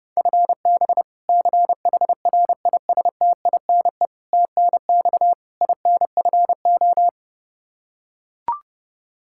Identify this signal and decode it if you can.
Morse code, heard as a keyed tone, about 30 wpm, 700 Hz. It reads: FB CHRISTINE TNX INFO
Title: Practice QSO - Exchange 3 - More Info - 30wpm